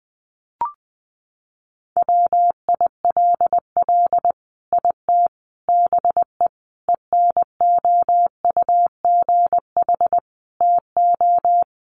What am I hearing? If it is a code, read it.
Morse code, heard as a keyed tone, about 20 wpm, 700 Hz. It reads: WILL IT BE ENOUGH TO
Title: Sentences from Top 1100 Words 20wpm